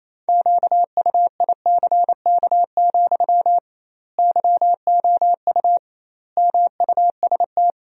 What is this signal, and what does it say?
Morse code, heard as a keyed tone, about 28 wpm, 700 Hz. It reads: QUICK, YOU MUST
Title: 2x Word-Spacing - Sentences from Top 500 Words - 28wpm